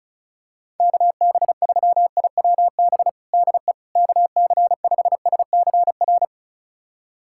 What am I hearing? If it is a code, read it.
Morse code, heard as a keyed tone, about 34 wpm, 690 Hz. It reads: KB3IWB DE KC5SCR
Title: USA Callsign DE Callsign - 6 Characters - 35wpm